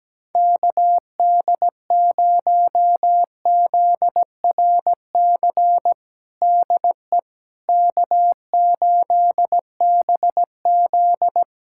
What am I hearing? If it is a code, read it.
Morse code, heard as a keyed tone, about 17 wpm, 700 Hz. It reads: KD0ZRC DE K8BZ
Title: Practice QSO - Exchange 4 - Ending - 17wpm